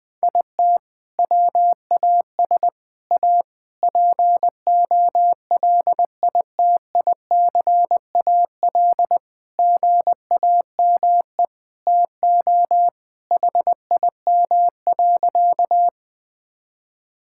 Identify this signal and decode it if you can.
Morse code, heard as a keyed tone, about 21 wpm, 700 Hz. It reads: IT WAS A POLITICAL GAME TO HIM.
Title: Sentences from Top 1000 Words - Review Part 5 - 20wpm